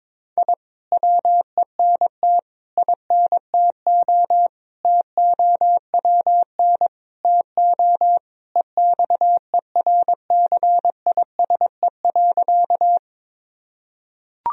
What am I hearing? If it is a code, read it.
Morse code, heard as a keyed tone, about 22 wpm, 700 Hz. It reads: I WENT INTO TOWN TO EXERCISE.
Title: Sentences from Top 1000 Words - Review Part 4 - 22wpm